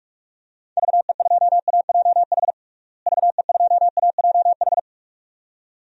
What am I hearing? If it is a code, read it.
Morse code, heard as a keyed tone, about 41 wpm, 700 Hz. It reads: VE2AJH VE2AJH
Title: DXCC-100 Call Signs - 6 Characters - Repeated 2x - 45wpm